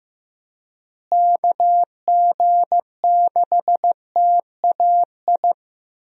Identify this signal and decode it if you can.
Morse code, heard as a keyed tone, about 15 wpm, 700 Hz. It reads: KG6TAI
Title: USA Call Signs - 6 Characters - Repeated 2x - 15wpm